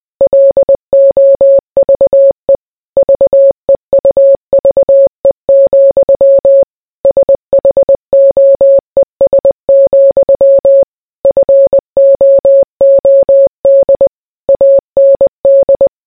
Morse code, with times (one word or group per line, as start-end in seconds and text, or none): 0.2-2.6 LOVE
3.0-6.6 VEUVE,
7.0-10.8 SHOES,
11.2-14.1 FOOD
14.5-15.9 AND